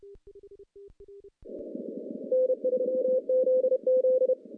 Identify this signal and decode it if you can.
Morse code, heard as a keyed tone, about 32 wpm, 510 Hz. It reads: N3ZZ
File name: contests / CW-SS-k5tr- -0400